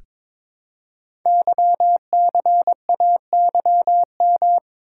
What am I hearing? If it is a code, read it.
Morse code, heard as a keyed tone, about 22 wpm, 700 Hz. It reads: YCAYM
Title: Code Groups - 5 Letters - 22wpm